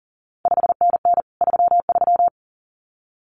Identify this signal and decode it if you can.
Morse code, heard as a keyed tone, about 39 wpm, 700 Hz. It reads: 5NN 33
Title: CQ WW DX Contest Exchange Encore - 40wpm